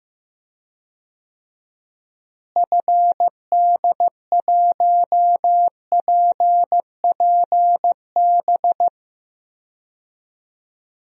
Morse code, heard as a keyed tone, about 15 wpm, 700 Hz. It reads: FD1PPB
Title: DXCC-100 Call Signs - 6 Characters - Repeated 3x - 15wpm